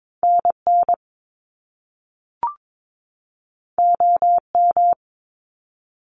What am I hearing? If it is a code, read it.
Morse code, heard as a keyed tone, about 22 wpm, 700 Hz. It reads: NN OM
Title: Top Two Letter Combinations in Words 22wpm